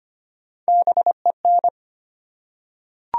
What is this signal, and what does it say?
Morse code, heard as a keyed tone, about 25 wpm, 700 Hz. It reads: BEN
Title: USA Common Names - Max 4 Letters - 25wpm